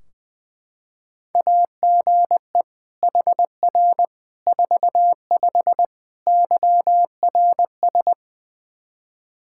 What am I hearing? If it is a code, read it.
Morse code, heard as a keyed tone, about 20 wpm, 700 Hz. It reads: AGE HR 45 YRS
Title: Practice QSO - Exchange 3 - More Info - 20wpm